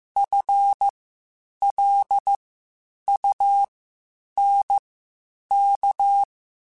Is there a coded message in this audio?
Yes: Morse code from 0.2 to 6.2 s, heard as a keyed tone, about 15 wpm, spaced as for 8 wpm, 800 Hz. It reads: FLUNK